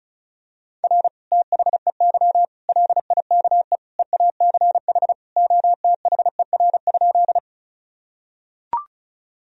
Morse code, heard as a keyed tone, about 34 wpm, 700 Hz. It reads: R THEY LIKE EACH OTHER?
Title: Sentences from Top 200 Words and Abbreviations 35wpm